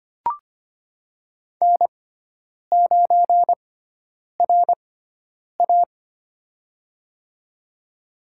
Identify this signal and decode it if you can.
Morse code, heard as a keyed tone, about 25 wpm, spaced as for 9 wpm, 700 Hz. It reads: N9RA
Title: USA Call Signs - 4 Characters - 6x Character Spacing - 25wpm